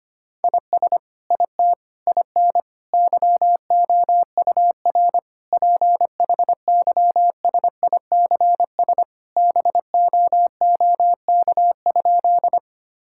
Morse code, heard as a keyed tone, about 25 wpm, 700 Hz. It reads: IS IT IN YOUR PHYSICS BOOK?
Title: Sentences from Top 1000 Words - Review Part 5 - 25wpm